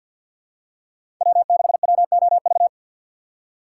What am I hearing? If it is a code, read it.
Morse code, heard as a keyed tone, about 50 wpm, 700 Hz. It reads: W6WOV
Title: USA Call Signs - 5 Characters - 4x Character Spacing - 50wpm